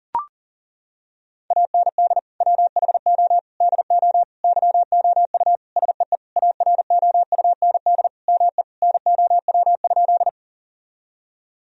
Morse code, heard as a keyed tone, about 37 wpm, 700 Hz. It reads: AND WHO DO YOU SEE AROUND ME NOW?
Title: Sentences from Top 1000 Words - Review Part 3 - 40wpm